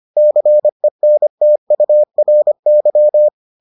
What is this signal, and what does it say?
Morse code, heard as a keyed tone, about 25 wpm, 600 Hz. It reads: CENTURY